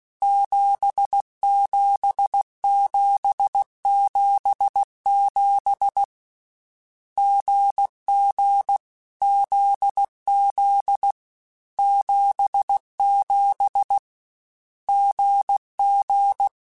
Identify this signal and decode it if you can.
Morse code, heard as a keyed tone, about 16 wpm, 800 Hz. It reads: 77777 GG ZZ 77 GG
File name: Difficult Combos / Difficult Combos-mp3s / GZ GZ7 @15